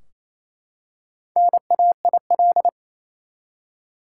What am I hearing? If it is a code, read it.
Morse code, heard as a keyed tone, about 28 wpm, 700 Hz. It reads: NAIL
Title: Four Letter Words - 28wpm